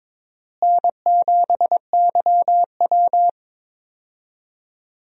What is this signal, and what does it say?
Morse code, heard as a keyed tone, about 23 wpm, 700 Hz. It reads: N7YW
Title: USA Call Signs - 4 Characters - Repeated 3x - 22wpm